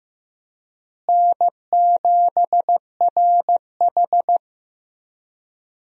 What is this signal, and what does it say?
Morse code, heard as a keyed tone, about 16 wpm, 700 Hz. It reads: N7RH
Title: USA Call Signs - 4 Characters - Repeated 3x - 15wpm